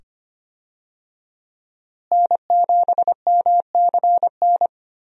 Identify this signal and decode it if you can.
Morse code, heard as a keyed tone, about 25 wpm, 700 Hz. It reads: N7MCN
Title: USA Call Signs - 5 Characters - Repeated 2x - 25wpm